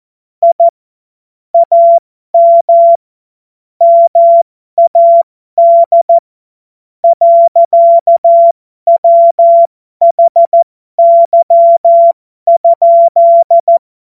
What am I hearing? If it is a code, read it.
Morse code, heard as a keyed tone, about 14 wpm, spaced as for 11 wpm, 680 Hz. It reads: I AM MAD. WHY?
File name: Morsekurs / Witze